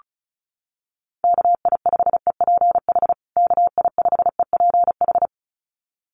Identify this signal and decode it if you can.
Morse code, heard as a keyed tone, about 34 wpm, 700 Hz. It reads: KI5EPH KI5EPH
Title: USA Call Signs Repeated 2x 35wpm